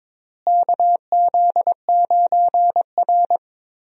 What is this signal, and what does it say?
Morse code, heard as a keyed tone, about 22 wpm, 700 Hz. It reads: KZ9R
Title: USA Call Signs - 4 Characters - Repeated 3x - 22wpm